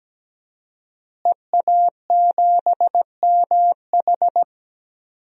Morse code, heard as a keyed tone, about 17 wpm, 700 Hz. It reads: EA7MH
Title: DXCC-100 Call Signs - 5 Characters - Repeated 2x - 17wpm